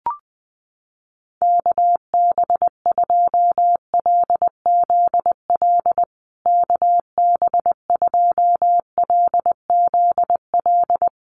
Morse code, heard as a keyed tone, about 20 wpm, 700 Hz. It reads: KB2LZL KB2LZL
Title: USA Call Signs Repeated 3x 20wpm